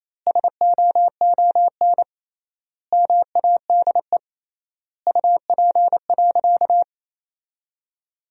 Morse code, heard as a keyed tone, about 28 wpm, 700 Hz. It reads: SOON MADE UP.
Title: 3x Word-Spacing - Sentences from Top 500 Words - 28wpm